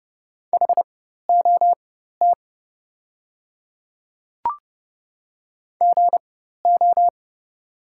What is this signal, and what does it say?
Morse code, heard as a keyed tone, about 30 wpm, spaced as for 14 wpm, 700 Hz. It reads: HOT GO
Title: 4x Character Spacing - Top 100 Words - 30wpm